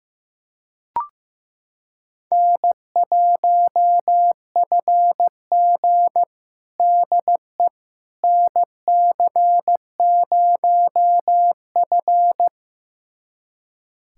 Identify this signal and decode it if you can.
Morse code, heard as a keyed tone, about 16 wpm, 700 Hz. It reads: N1FG DE NC0F
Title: USA Callsign DE Callsign - 4 Characters - 15wpm